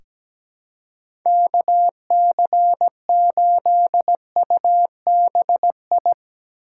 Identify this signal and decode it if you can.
Morse code, heard as a keyed tone, about 17 wpm, 700 Hz. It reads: KC8UBI
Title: USA Call Signs - 2x Character Spacing - 17wpm